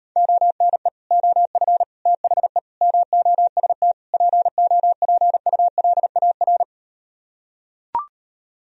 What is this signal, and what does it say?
Morse code, heard as a keyed tone, about 38 wpm, 700 Hz. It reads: ONE OF THE MOST POPULAR